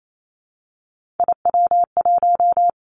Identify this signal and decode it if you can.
Morse code, heard as a keyed tone, about 28 wpm, 700 Hz. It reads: IW1
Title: Three Challenging Letter-Number Combinations 28wpm